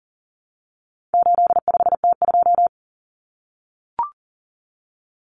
Morse code, heard as a keyed tone, about 40 wpm, 700 Hz. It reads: 85T2
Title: Common Cut Numbers - 4-Digit 40wpm